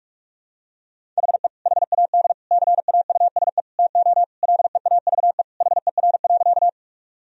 Morse code, heard as a keyed tone, about 45 wpm, 700 Hz. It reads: HE HAD CAUSE TO LEAVE HER.
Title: Sentences from Top 1000 Words - Review Part 1 - 45wpm